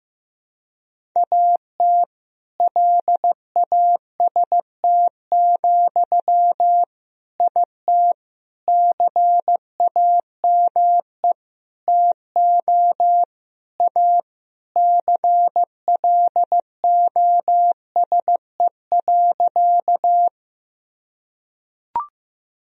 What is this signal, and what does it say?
Morse code, heard as a keyed tone, about 15 wpm, 700 Hz. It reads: AT LAST, IT CAME TO A CLOSE.
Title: Sentences from Top 1000 Words - Review Part 1 - 15wpm